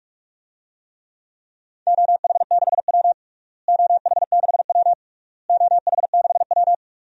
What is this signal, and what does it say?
Morse code, heard as a keyed tone, about 46 wpm, 700 Hz. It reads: OH6W OH6W OH6W
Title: DXCC-100 Call Signs - 4 Characters - Repeated 3x - 45wpm